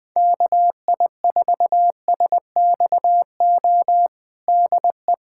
Morse code, heard as a keyed tone, about 21 wpm, 700 Hz. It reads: KI4SXO DE
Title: USA Callsign DE Callsign 20wpm